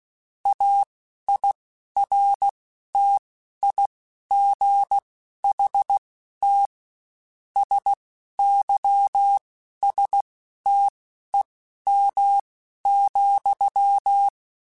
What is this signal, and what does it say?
Morse code, heard as a keyed tone, about 16 wpm, spaced as for 11 wpm, 800 Hz. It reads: AIRTIGHT SYSTEM,
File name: Quotes Spaced / Quotes Spaced-mp3s / Airtight sp @15